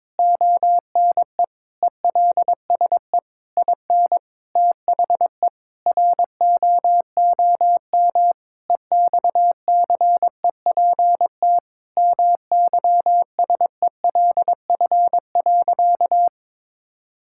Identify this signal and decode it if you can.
Morse code, heard as a keyed tone, about 22 wpm, 700 Hz. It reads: ONE ELSE IN THE ROOM EXCEPT MYSELF.
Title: Sentences from Top 1600 Words 22wpm